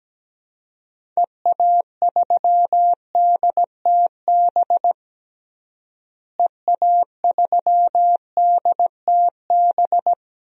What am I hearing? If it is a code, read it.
Morse code, heard as a keyed tone, about 17 wpm, 700 Hz. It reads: EA3DTB EA3DTB
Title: DXCC-100 Call Signs - 6 Characters - Repeated 2x - 17wpm